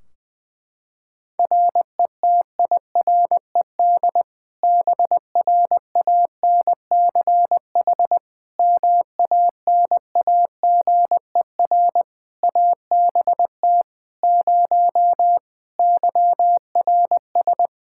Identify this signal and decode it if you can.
Morse code, heard as a keyed tone, about 20 wpm, 700 Hz. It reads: RETIRED BRANCH MANAGER ABT 0 YRS
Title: Practice QSO - Exchange 3 - More Info - 20wpm